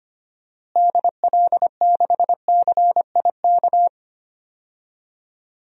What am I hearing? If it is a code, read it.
Morse code, heard as a keyed tone, about 25 wpm, 700 Hz. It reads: DL6CIK
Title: DXCC-100 Call Signs - 6 Characters - Repeated 3x - 25wpm